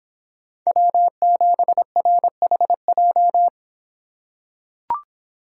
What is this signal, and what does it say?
Morse code, heard as a keyed tone, about 26 wpm, 700 Hz. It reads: W7RHJ